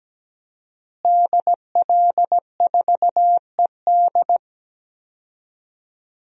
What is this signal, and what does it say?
Morse code, heard as a keyed tone, about 17 wpm, 700 Hz. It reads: DL4ED
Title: DXCC-100 Call Signs - 5 Characters - Repeated 2x - 17wpm